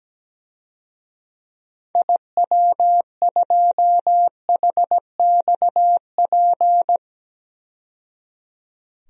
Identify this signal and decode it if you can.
Morse code, heard as a keyed tone, about 17 wpm, 700 Hz. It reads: IW2HXP
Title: DXCC-100 Call Signs - 6 Characters - 17wpm